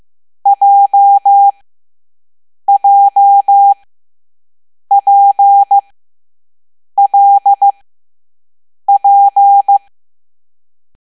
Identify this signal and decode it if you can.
Morse code, heard as a keyed tone, about 15 wpm, spaced as for 6 wpm, 800 Hz. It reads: JJPLP
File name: Morse / Level7B